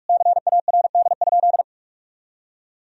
Morse code, heard as a keyed tone, about 41 wpm, 700 Hz. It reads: CARD?
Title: Sentences from Top 1000 Words - Review Part 4 - 45wpm